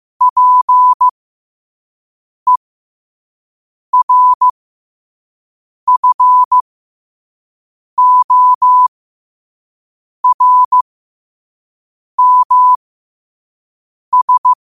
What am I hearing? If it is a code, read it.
Morse code, heard as a keyed tone, about 15 wpm, spaced as for 5 wpm, 1000 Hz. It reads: PERFORMS
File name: qotdfiles / 5wpm